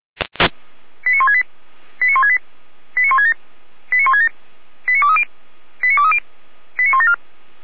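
Five-tone selective call (ZVEI), heard as quick runs of five tones, tones of about 80 ms. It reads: 89168 89168 89167 89167 89230 89230 89174
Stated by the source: ZVEI five-tone selective call 89120 received on SDR.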